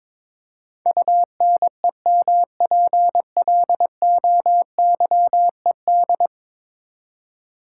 Morse code, heard as a keyed tone, about 22 wpm, 700 Hz. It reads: UNEMPLOYED
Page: morse-code-ninja-practice-22wpm